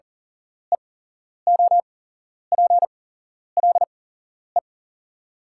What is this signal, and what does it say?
Morse code, heard as a keyed tone, about 40 wpm, spaced as for 11 wpm, 690 Hz. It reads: EOPLE
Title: 8x Character Spacing - Top 100 Words - 40wpm